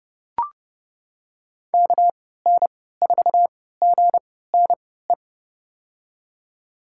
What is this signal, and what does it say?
Morse code, heard as a keyed tone, about 30 wpm, spaced as for 17 wpm, 700 Hz. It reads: KN4GNE
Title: USA Call Signs - 6 Characters - 3x Character Spacing - 30wpm